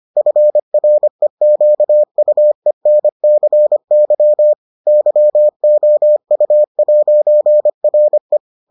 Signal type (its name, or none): Morse code